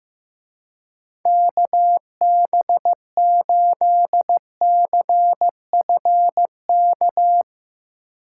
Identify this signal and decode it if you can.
Morse code, heard as a keyed tone, about 15 wpm, 700 Hz. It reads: KB8CFK
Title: Speed Racing - USA 2x3 Call Signs 15wpm